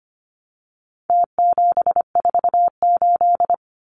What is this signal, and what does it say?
Morse code, heard as a keyed tone, about 25 wpm, 700 Hz. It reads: T748
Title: Numbers - Contest Style - 4-Digit 25wpm